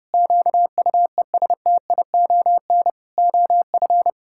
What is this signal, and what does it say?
Morse code, heard as a keyed tone, about 30 wpm, 700 Hz. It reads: QUESTION OF
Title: Sentences from Top 1600 Words 30wpm